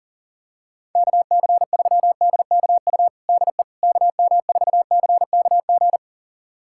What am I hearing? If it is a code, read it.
Morse code, heard as a keyed tone, about 37 wpm, 700 Hz. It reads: KC3DKU DE KM4CKG
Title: USA Callsign DE Callsign 40wpm